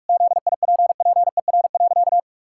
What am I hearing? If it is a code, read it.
Morse code, heard as a keyed tone, about 44 wpm, 700 Hz. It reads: ZIPPER.
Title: Pangram Challenge v2 - 45wpm